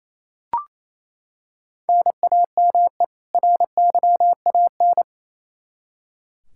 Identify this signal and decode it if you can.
Morse code, heard as a keyed tone, about 28 wpm, 700 Hz. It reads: NAME RYAN
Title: Practice QSO - Exchange 1 - Beginning - 28wpm